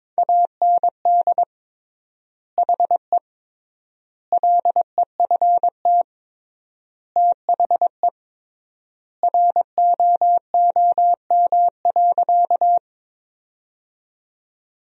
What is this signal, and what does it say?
Morse code, heard as a keyed tone, about 22 wpm, 700 Hz. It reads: AND HE LEFT THE ROOM.
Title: 3x Word-Spacing - Sentences from Top 300 Words - 22wpm